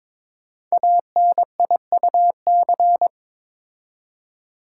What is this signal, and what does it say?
Morse code, heard as a keyed tone, about 22 wpm, 700 Hz. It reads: ANIUC
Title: Code Groups - 5 Letter:Numbers - 22wpm